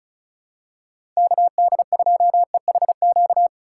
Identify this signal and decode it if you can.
Morse code, heard as a keyed tone, about 36 wpm, 700 Hz. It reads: KD2EHQ
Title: USA Call Signs - 6 Characters - 4x Character Spacing - 35wpm